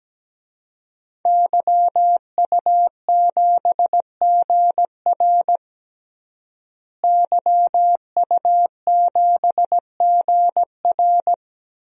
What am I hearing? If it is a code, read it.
Morse code, heard as a keyed tone, about 17 wpm, 700 Hz. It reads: YU7GR YU7GR
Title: DXCC-100 Call Signs - 5 Characters - Repeated 2x - 17wpm